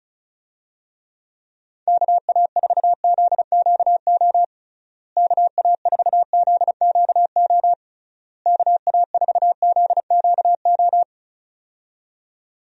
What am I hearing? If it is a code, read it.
Morse code, heard as a keyed tone, about 34 wpm, 700 Hz. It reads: KA4ZQO KA4ZQO KA4ZQO
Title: USA Call Signs - 6 Characters - Repeated 3x - 35wpm